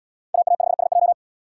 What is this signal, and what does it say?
Morse code, heard as a keyed tone, about 72 wpm, 700 Hz. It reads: AA5SJ